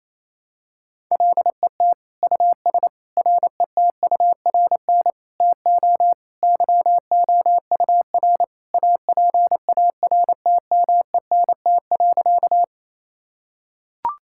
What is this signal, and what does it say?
Morse code, heard as a keyed tone, about 27 wpm, 700 Hz. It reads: LET US RETURN TO YOUR APARTMENT.
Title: Sentences from Top 1000 Words - Review Part 6 - 28wpm